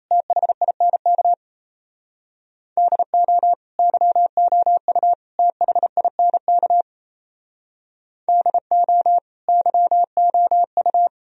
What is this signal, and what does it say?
Morse code, heard as a keyed tone, about 34 wpm, 700 Hz. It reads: THINK DO YOU THINK DO YOU
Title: Speed-Racing - Sets of 3 Words 28wpm